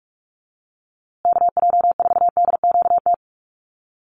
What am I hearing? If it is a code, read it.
Morse code, heard as a keyed tone, about 44 wpm, 700 Hz. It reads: KJ4DQT
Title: USA Call Signs Repeated 3x 45wpm